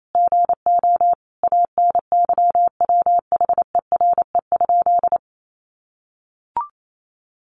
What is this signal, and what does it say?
Morse code, heard as a keyed tone, about 28 wpm, 700 Hz. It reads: GO ANYWHERE?
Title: Sentences from Top 1400 Words 28wpm